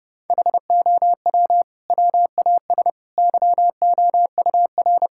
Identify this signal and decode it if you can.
Morse code, heard as a keyed tone, about 32 wpm, 700 Hz. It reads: HOW WAS YOUR
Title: Sentences from Top 1600 Words 30wpm